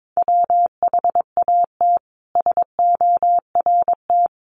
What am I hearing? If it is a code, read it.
Morse code, heard as a keyed tone, about 22 wpm, 700 Hz. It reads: WHAT SORT
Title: Sentences from Top 1300 Words 22wpm